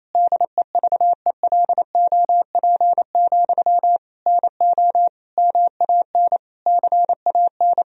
Morse code, heard as a keyed tone, about 28 wpm, 700 Hz. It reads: DEVELOP, NO MAN CAN
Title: Sentences from Top 1000 Words - Review Part 2 - 28wpm